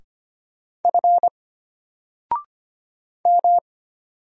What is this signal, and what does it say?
Morse code, heard as a keyed tone, about 25 wpm, 700 Hz. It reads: F M
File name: Single-Letter-Number-25wpm